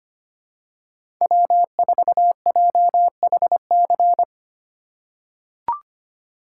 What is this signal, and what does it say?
Morse code, heard as a keyed tone, about 25 wpm, 700 Hz. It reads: W4JHC
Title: USA Call Signs - 5 Characters - 4x Character Spacing - 25wpm